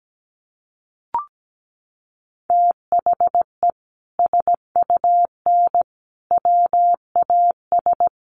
Morse code, heard as a keyed tone, about 17 wpm, 700 Hz. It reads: THE SUN WAS
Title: Sentences from Top 1300 Words 17wpm